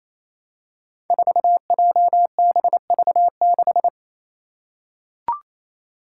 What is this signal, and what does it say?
Morse code, heard as a keyed tone, about 28 wpm, 700 Hz. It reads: 4JBV6
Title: Five Challenging Letter-Number Combinations 28wpm